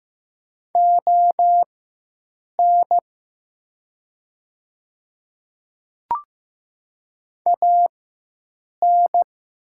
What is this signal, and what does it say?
Morse code, heard as a keyed tone, about 15 wpm, spaced as for 7 wpm, 700 Hz. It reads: ON AN
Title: 4x Character Spacing - Top 100 Words - 15wpm